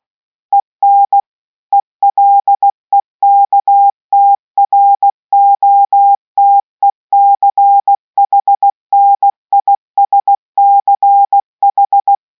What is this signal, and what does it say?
Morse code, heard as a keyed tone, about 16 wpm, 800 Hz. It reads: EN ELEKTROTECHNISCH